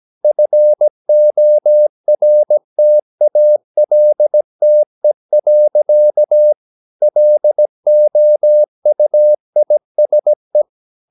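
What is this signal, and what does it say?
Morse code, heard as a keyed tone, about 17 wpm, 600 Hz. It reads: FORTALTE. LOUISE
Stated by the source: Slow (QRS) Morse practice at 17 wpm, from CW med Gnister.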